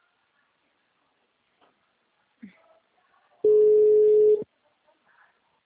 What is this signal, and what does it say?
Ringback tone (Europe, 425 Hz).